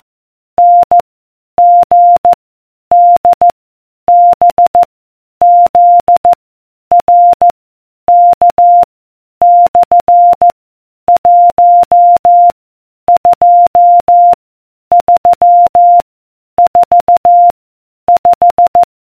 Morse code, heard as a keyed tone, about 14 wpm, 700 Hz. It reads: N G D B Z R K / 1 2 3 4 5